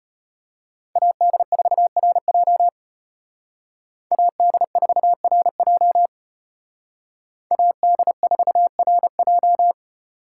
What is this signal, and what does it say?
Morse code, heard as a keyed tone, about 33 wpm, 700 Hz. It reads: AD4RJ AD4RJ AD4RJ